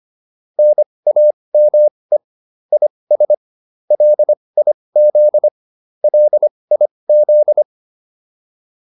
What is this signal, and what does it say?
Morse code, heard as a keyed tone, about 25 wpm, spaced as for 20 wpm, 600 Hz. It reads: NAME IS LIZ LIZ
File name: QSO304 20